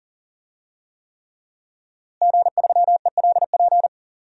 Morse code, heard as a keyed tone, about 40 wpm, 700 Hz. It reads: G3ELP
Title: DXCC-100 Call Signs - 5 Characters - Repeated 3x - 40wpm